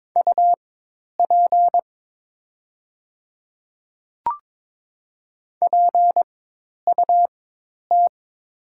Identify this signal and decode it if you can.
Morse code, heard as a keyed tone, about 22 wpm, spaced as for 10 wpm, 700 Hz. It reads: UP PUT
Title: 4x Character Spacing - Top 300 Words - 22wpm